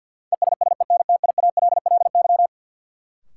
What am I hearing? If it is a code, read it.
Morse code, heard as a keyed tone, about 53 wpm, 700 Hz. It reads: ESSENTIALLY
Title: Sentences from Top 1300 Words - New Word Drill Alt - 50wpm